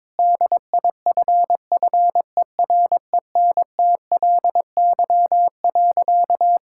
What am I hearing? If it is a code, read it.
Morse code, heard as a keyed tone, about 22 wpm, 700 Hz. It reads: DIFFERENTLY.
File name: practice / 22wpm / Sentences with 3-Word Phrases - 22wpm